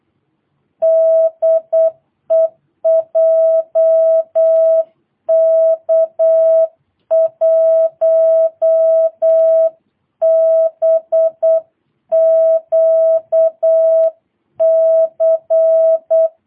Morse code, heard as a keyed tone, about 8 wpm, 650 Hz. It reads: DE JK1BQC
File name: jh8cbh / 2022cw